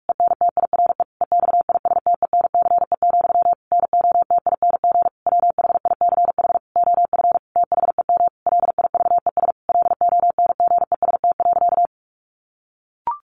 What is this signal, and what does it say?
Morse code, heard as a keyed tone, about 44 wpm, 700 Hz. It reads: ENTIRE EXISTENCE, NOTING WHICH OF THEM LIVES LONGEST.
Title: A Sound Of Thunder 45wpm